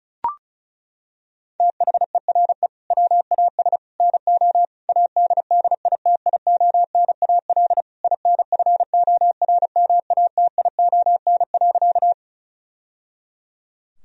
Morse code, heard as a keyed tone, about 36 wpm, 700 Hz. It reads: THERE WAS NO ADDITIONAL INFORMATION.
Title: Sentences from Top 1000 Words - Review Part 4 - 35wpm